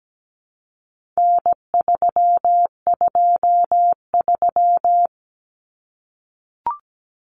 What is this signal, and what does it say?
Morse code, heard as a keyed tone, about 17 wpm, 700 Hz. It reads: N323
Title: Common Cut Numbers - 4-Digit 17wpm